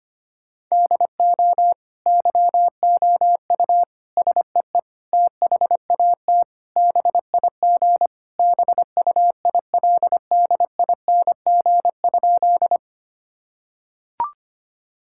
Morse code, heard as a keyed tone, about 25 wpm, 700 Hz. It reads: DO YOU SEE THAT BIG BUILDING?
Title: Sentences from Top 1000 Words - Review Part 5 - 25wpm